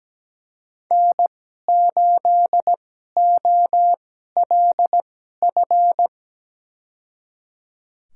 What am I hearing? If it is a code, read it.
Morse code, heard as a keyed tone, about 17 wpm, spaced as for 12 wpm, 700 Hz. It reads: N8OLF
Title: USA Call Signs - 5 Characters - 2x Character Spacing - 17wpm